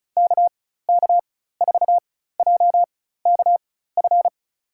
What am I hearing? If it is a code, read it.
Morse code, heard as a keyed tone, about 34 wpm, spaced as for 16 wpm, 700 Hz. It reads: KK4JKF
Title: USA Call Signs - 6 Characters - 4x Character Spacing - 35wpm